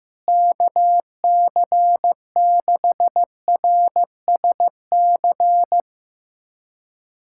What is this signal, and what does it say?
Morse code, heard as a keyed tone, about 15 wpm, 700 Hz. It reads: KC6RSC
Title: USA Callsign DE Callsign - 6 Characters - 15wpm